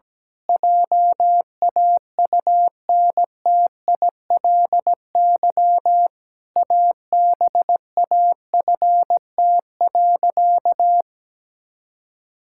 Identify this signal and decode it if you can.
Morse code, heard as a keyed tone, about 17 wpm, 700 Hz. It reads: JAUNTILY ABAFT.
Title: Pangram Challenge v2 - 17wpm